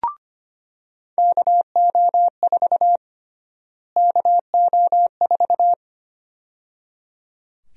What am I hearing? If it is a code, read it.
Morse code, heard as a keyed tone, about 25 wpm, 700 Hz. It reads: KO4 KO4
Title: USA Call Sign Prefixes - Repeated 2x - 25wpm